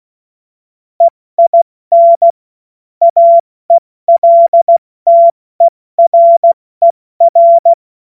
Morse code, heard as a keyed tone, about 16 wpm, 680 Hz. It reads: EIN AELTERER
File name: Morsekurs / Witze / Witz 14.November 2023 70 F80